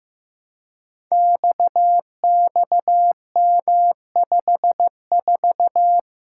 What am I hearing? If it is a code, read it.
Morse code, heard as a keyed tone, about 15 wpm, 700 Hz. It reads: XXM54
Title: Code Groups - 5 Letter:Numbers - 15wpm